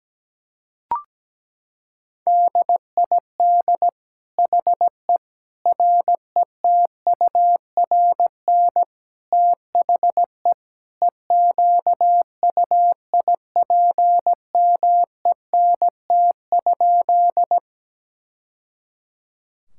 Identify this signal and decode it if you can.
Morse code, heard as a keyed tone, about 17 wpm, 700 Hz. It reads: DID HE RETURN THE EQUIPMENT?
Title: Sentences from Top 1000 Words - Review Part 5 - 17wpm